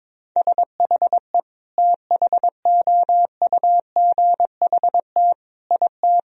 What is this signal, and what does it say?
Morse code, heard as a keyed tone, about 21 wpm, 700 Hz. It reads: SHE THOUGHT IT
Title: Sentences from Top 1000 Words - Review Part 1 - 22wpm